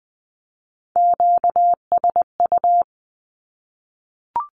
Morse code, heard as a keyed tone, about 20 wpm, 700 Hz. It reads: QSU
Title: Q-Codes 20wpm